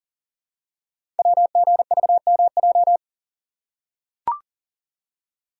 Morse code, heard as a keyed tone, about 42 wpm, 700 Hz. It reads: WGVMJ
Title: Code Groups - 5 Letters - 40wpm